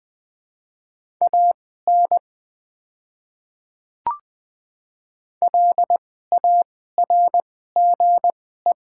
Morse code, heard as a keyed tone, about 20 wpm, spaced as for 15 wpm, 700 Hz. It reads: AN LARGE
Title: 2x Character Spacing - Top 200 Words - 20wpm